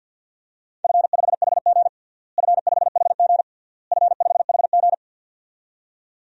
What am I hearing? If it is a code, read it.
Morse code, heard as a keyed tone, about 49 wpm, 700 Hz. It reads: F5HG F5HG F5HG
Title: DXCC-100 Call Signs - 4 Characters - Repeated 3x - 50wpm